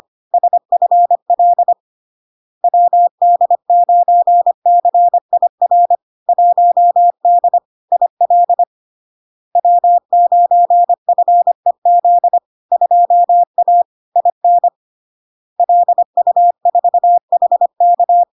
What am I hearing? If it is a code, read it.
Morse code, heard as a keyed tone, about 25 wpm, 700 Hz. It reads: SFL WD9CIR 1D IL W9FEZ 2A IN LU4HK
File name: LARK-FD-2022-Log-in-CW-25wpm-700-Hz